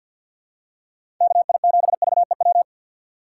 Morse code, heard as a keyed tone, about 50 wpm, 700 Hz. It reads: KI7VEW
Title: USA Call Signs - 6 Characters - 4x Character Spacing - 50wpm